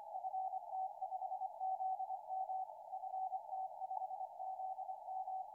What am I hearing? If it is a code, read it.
Morse code, heard as a keyed tone, about 25 wpm, 760 Hz. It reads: FT5ZM FT5ZM
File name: ft5zm 01-02-2014 1803ut 160m